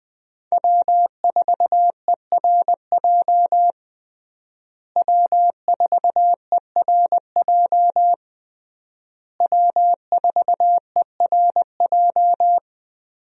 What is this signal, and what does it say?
Morse code, heard as a keyed tone, about 20 wpm, 700 Hz. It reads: W4ERJ W4ERJ W4ERJ
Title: USA Call Signs - 5 Characters - Repeated 3x - 20wpm